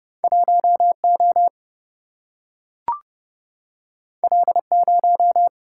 Morse code, heard as a keyed tone, about 30 wpm, 700 Hz. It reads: 1O L0
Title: Single-Letter Single-Number 30wpm